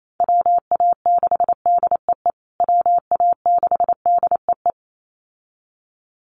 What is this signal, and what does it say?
Morse code, heard as a keyed tone, about 28 wpm, 700 Hz. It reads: WA6DEE WA6DEE
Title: USA Call Signs Repeated 2x 28wpm